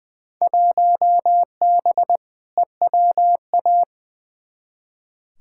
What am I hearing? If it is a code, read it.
Morse code, heard as a keyed tone, about 20 wpm, 700 Hz. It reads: 1B EWA
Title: Field Day Exchanges Encore - 20wpm